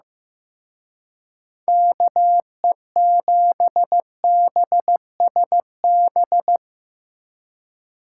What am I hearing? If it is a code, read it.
Morse code, heard as a keyed tone, about 17 wpm, 700 Hz. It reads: KE7BSB
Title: Speed Racing - USA 2x3 Call Signs 15wpm